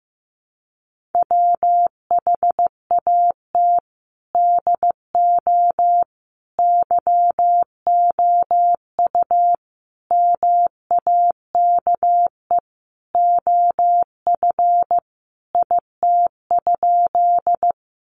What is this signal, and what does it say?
Morse code, heard as a keyed tone, about 15 wpm, 700 Hz. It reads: WHAT DO YOU MAKE OF IT?
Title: Sentences from Top 100 Words 15wpm